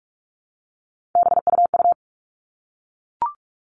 Morse code, heard as a keyed tone, about 44 wpm, 700 Hz. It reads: BUU
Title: All Cut Numbers - 3-Digit 45wpm